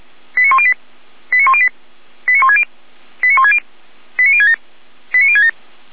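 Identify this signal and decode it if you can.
Five-tone selective call (ZVEI), heard as quick runs of five tones, tones of about 70 ms. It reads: 89108 89108 89160 89160 89067 89067
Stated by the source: Five-tone ZVEI selective call received on SDR, from the feuerwehr-orlamuende.org fire brigade audio archive.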